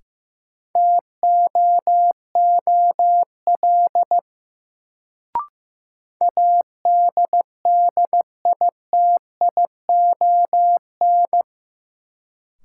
Morse code, heard as a keyed tone, about 15 wpm, 700 Hz. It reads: TOOL ADDITION
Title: Sentences from Top 1000 Words - Review Part 6 - New Word Drill - 15wpm